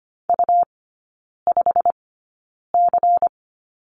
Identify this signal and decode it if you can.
Morse code, heard as a keyed tone, about 25 wpm, spaced as for 9 wpm, 700 Hz. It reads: U5C
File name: Lesson 17e - Letter u - Three Characters 25wpm